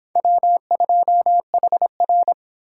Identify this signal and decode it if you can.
Morse code, heard as a keyed tone, about 26 wpm, 700 Hz. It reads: W2HR